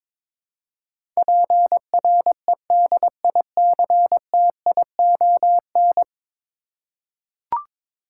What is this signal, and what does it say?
Morse code, heard as a keyed tone, about 22 wpm, 700 Hz. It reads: PREDICTION
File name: practice / 22wpm / 10 Letter Words - First 3 Letters Given - 22wpm